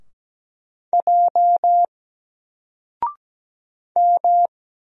Morse code, heard as a keyed tone, about 17 wpm, 700 Hz. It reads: J M
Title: Single Character (Letter:Number:Punctuation:Prosign) Alt 17wpm